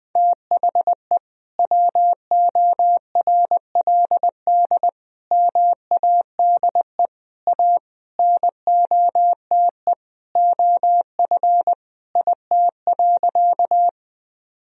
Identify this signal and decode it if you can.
Morse code, heard as a keyed tone, about 21 wpm, 700 Hz. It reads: THE WORLD MADE A NOTE OF IT.
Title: Sentences from Top 1000 Words - Review Part 2 - 20wpm